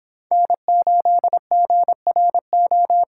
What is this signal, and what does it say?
Morse code, heard as a keyed tone, about 26 wpm, 700 Hz. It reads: N8GRO
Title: Speed Racing - USA 1x3 Call Signs 22wpm